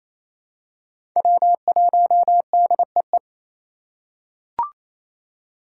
Morse code, heard as a keyed tone, about 27 wpm, 700 Hz. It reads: W1DEE
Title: USA Call Signs - 5 Characters - 4x Character Spacing - 28wpm